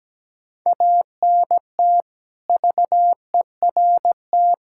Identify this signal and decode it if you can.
Morse code, heard as a keyed tone, about 17 wpm, 700 Hz. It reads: ANT VERT
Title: K1USN SST QSO Practice 17wpm